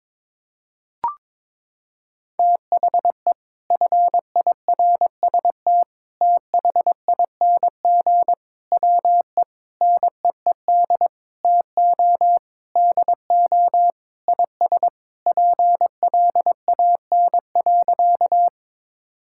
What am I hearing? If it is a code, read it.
Morse code, heard as a keyed tone, about 22 wpm, 700 Hz. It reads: THE FIRST THING WE NEED TO DO IS PLAN.
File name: practice / 22wpm / Sentences with 8-Word Phrases - 22wpm